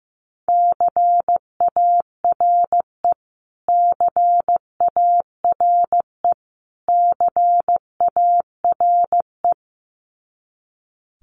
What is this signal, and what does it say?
Morse code, heard as a keyed tone, about 15 wpm, 700 Hz. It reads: CARE CARE CARE
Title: Top 300 Words Repeated 15wpm